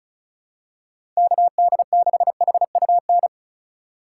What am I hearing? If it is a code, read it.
Morse code, heard as a keyed tone, about 34 wpm, 700 Hz. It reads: KD6HUN
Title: USA Call Signs - 6 Characters - 2x Character Spacing - 35wpm